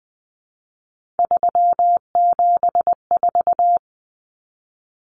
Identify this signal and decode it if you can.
Morse code, heard as a keyed tone, about 20 wpm, 700 Hz. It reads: 374
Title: Numbers - 3-Digit 20wpm